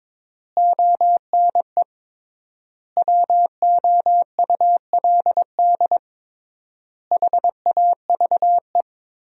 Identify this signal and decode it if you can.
Morse code, heard as a keyed tone, about 22 wpm, 700 Hz. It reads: ONE WOULD HAVE
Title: 3x Word-Spacing - Sentences from Top 300 Words - 22wpm